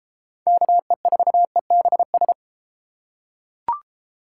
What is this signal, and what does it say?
Morse code, heard as a keyed tone, about 33 wpm, 700 Hz. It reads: KE4EBS